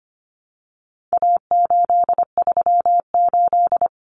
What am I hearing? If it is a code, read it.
Morse code, heard as a keyed tone, about 25 wpm, 700 Hz. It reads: A838
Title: Common Cut Numbers - 4-Digit 25wpm